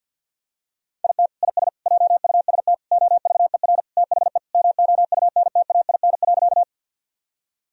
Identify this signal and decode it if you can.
Morse code, heard as a keyed tone, about 52 wpm, 700 Hz. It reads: IT IS JUST OVER THE MOUNTAIN.
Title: Sentences from Top 1000 Words - Review Part 1 - 50wpm